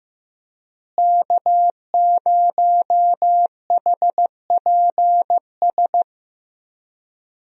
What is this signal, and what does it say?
Morse code, heard as a keyed tone, about 15 wpm, 700 Hz. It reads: K0HPS
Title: USA Call Signs - 5 Characters - Repeated 3x - 15wpm